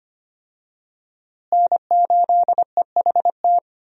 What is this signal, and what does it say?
Morse code, heard as a keyed tone, about 25 wpm, 700 Hz. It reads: N8EHT
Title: USA Call Signs - 5 Characters - Repeated 3x - 25wpm